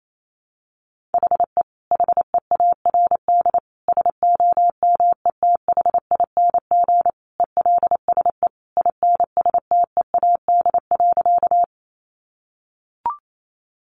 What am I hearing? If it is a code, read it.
Morse code, heard as a keyed tone, about 28 wpm, 700 Hz. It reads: HE HEARD SOMETHING ELSE INSTEAD.
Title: Sentences from Top 800 Words 28wpm